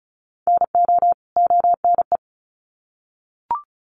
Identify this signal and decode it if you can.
Morse code, heard as a keyed tone, about 34 wpm, 700 Hz. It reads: NO ONE
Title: Sets of 2 Words 35wpm